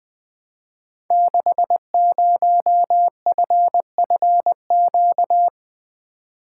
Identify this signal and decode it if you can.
Morse code, heard as a keyed tone, about 20 wpm, 700 Hz. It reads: 60FFQ
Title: Code Groups - 5 Letter:Numbers - 20wpm